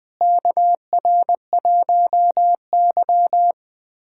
Morse code, heard as a keyed tone, about 21 wpm, 700 Hz. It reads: KR1Y
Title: USA Callsign DE Callsign - 4 Characters - 20wpm